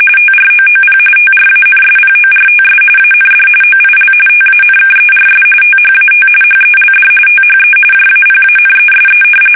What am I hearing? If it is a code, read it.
RTTY, about 45.45 baud. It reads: TROOP TRANSPORTS AND OIL TANKERS THAT HELD THE JAPANESE
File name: fsk850